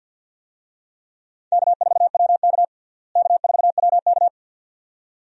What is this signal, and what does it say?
Morse code, heard as a keyed tone, about 53 wpm, 700 Hz. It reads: K4WK K4WK
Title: USA Call Signs - 4 Characters - Repeated 2x - 50wpm